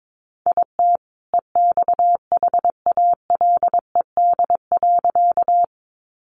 Morse code, heard as a keyed tone, about 22 wpm, 700 Hz. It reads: IT EXHALED.
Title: A Sound Of Thunder 22wpm